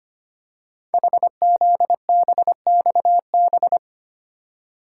Morse code, heard as a keyed tone, about 25 wpm, 700 Hz. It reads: HZBXB
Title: Code Groups - 5 Letters - 25wpm